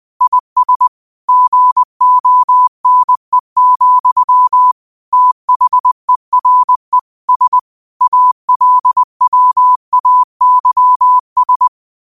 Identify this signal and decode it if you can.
Morse code, heard as a keyed tone, about 20 wpm, 1000 Hz. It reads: IS GONE, THERE S ALWAYS
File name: qotdfiles / 20wpm